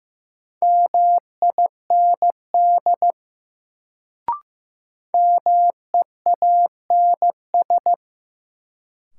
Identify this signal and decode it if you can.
Morse code, heard as a keyed tone, about 15 wpm, 700 Hz. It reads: MIND MEANS
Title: Sets of 5 Words - New Word Drill - 15wpm